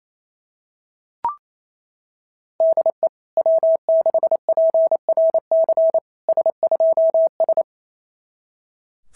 Morse code, heard as a keyed tone, about 29 wpm, 660 Hz. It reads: DE W6PRC S2S
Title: SOTA QSO Practice 28wpm